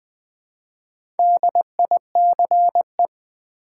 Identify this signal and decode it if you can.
Morse code, heard as a keyed tone, about 20 wpm, 700 Hz. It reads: DICE
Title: Easy Words with Hints - 20wpm